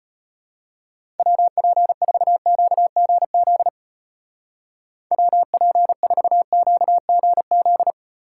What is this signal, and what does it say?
Morse code, heard as a keyed tone, about 31 wpm, 700 Hz. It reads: WP4QGZ WP4QGZ
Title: Speed Racing - USA Call Signs 25wpm